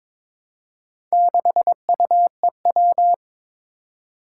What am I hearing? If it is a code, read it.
Morse code, heard as a keyed tone, about 22 wpm, 700 Hz. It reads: 6UEW
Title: Four Challenging Letter-Number Combinations 22wpm